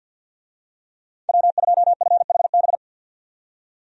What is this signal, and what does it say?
Morse code, heard as a keyed tone, about 52 wpm, 700 Hz. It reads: W2FHB
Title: USA Call Signs - 5 Characters - 50wpm